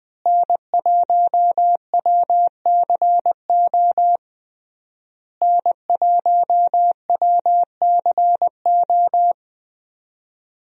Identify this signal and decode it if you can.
Morse code, heard as a keyed tone, about 20 wpm, 700 Hz. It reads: N1WCO N1WCO
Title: USA Call Signs - 5 Characters - Repeated 3x - 20wpm